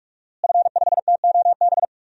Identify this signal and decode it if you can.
Morse code, heard as a keyed tone, about 42 wpm, 700 Hz. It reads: F5TOB
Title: DXCC-100 Call Signs - 5 Characters - Repeated 3x - 45wpm